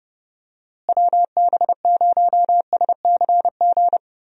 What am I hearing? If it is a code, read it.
Morse code, heard as a keyed tone, about 30 wpm, 700 Hz. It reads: WB0SCG
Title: USA Call Signs - 6 Characters - Repeated 2x - 30wpm